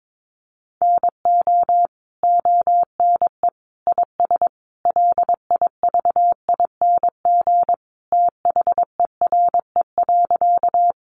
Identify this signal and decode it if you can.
Morse code, heard as a keyed tone, about 22 wpm, 700 Hz. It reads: NO ONE IS LIVING THERE.
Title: Sentences from Top 1100 Words 22wpm